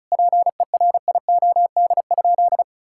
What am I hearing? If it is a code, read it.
Morse code, heard as a keyed tone, about 36 wpm, 700 Hz. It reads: PERIOD?
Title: Sentences from Top 1000 Words - Review Part 5 - 35wpm